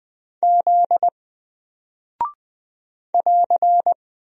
Morse code, heard as a keyed tone, about 20 wpm, 700 Hz. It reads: Z <AR>
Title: Single Character (Letter:Number:Punctuation:Prosign) 20wpm